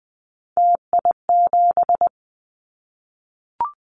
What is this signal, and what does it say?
Morse code, heard as a keyed tone, about 20 wpm, 700 Hz. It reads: TI7
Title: Callsign Prefixes for Top 100 DXCC - 20wpm